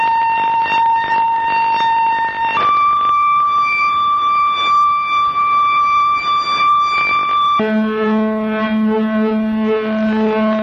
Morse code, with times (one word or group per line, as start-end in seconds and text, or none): none